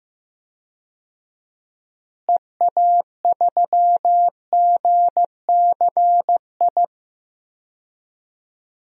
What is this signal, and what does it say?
Morse code, heard as a keyed tone, about 15 wpm, 700 Hz. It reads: EA3GCI
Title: DXCC-100 Call Signs - 6 Characters - Repeated 2x - 15wpm